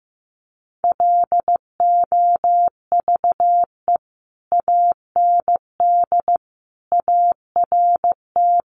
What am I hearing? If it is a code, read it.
Morse code, heard as a keyed tone, about 15 wpm, 700 Hz. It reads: LOVE AND ART
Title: Sentences from Top 1100 Words 15wpm